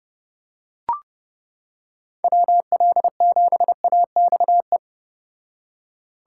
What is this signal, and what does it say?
Morse code, heard as a keyed tone, about 29 wpm, 700 Hz. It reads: WL7AXE